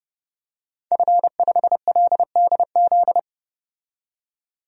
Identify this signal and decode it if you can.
Morse code, heard as a keyed tone, about 30 wpm, 700 Hz. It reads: F5LDZ
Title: DXCC-100 Call Signs - 5 Characters - 30wpm